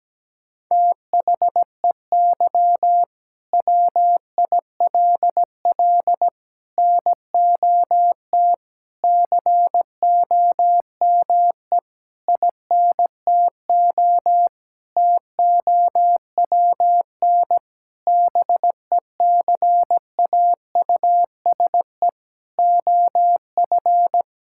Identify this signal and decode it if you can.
Morse code, heard as a keyed tone, about 17 wpm, 700 Hz. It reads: THEY WILL NOT COME INTO TOWN BECAUSE OF
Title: Sentences from Top 1000 Words - Review Part 5 - 17wpm